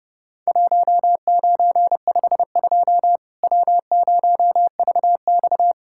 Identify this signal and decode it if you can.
Morse code, heard as a keyed tone, about 30 wpm, 700 Hz. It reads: 1952 W0VX
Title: Worked All Europe DX Contest - QTC Exchange - 30wpm